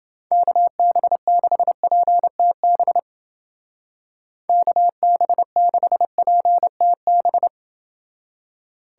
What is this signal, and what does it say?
Morse code, heard as a keyed tone, about 29 wpm, 700 Hz. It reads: KB6PTB KB6PTB